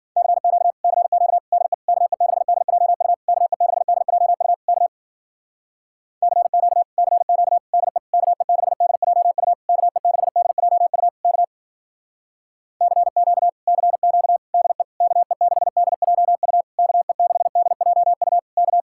Morse code, heard as a keyed tone, about 51 wpm, 700 Hz. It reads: CQ CQ DE KE6DJU KE6DJU K CQ CQ DE KE6DJU KE6DJU K CQ CQ DE KE6DJU KE6DJU K